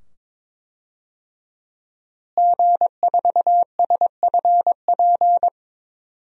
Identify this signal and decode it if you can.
Morse code, heard as a keyed tone, about 22 wpm, 700 Hz. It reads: G4SFP
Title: DXCC-100 Call Signs - 5 Characters - Repeated 2x - 22wpm